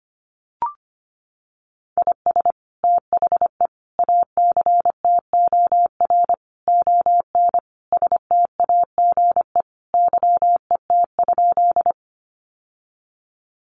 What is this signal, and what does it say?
Morse code, heard as a keyed tone, about 25 wpm, 700 Hz. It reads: IS THE ACTOR ON STAGE YET?
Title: Sentences from Top 1500 Words 25wpm